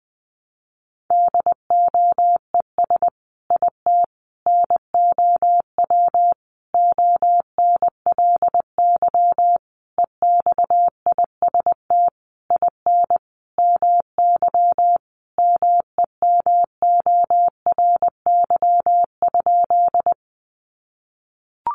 Morse code, heard as a keyed tone, about 20 wpm, 700 Hz. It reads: DOES IT NOW ONLY EXIST IN MY MEMORY?
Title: Sentences from Top 1300 Words 20wpm